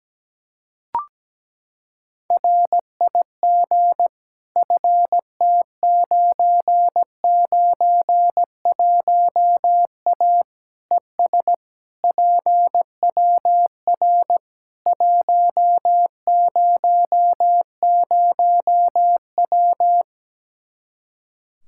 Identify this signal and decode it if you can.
Morse code, heard as a keyed tone, about 17 wpm, 700 Hz. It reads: RIG FT991A ES PWR 100W
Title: Practice QSO - Exchange 2 - Info - 17wpm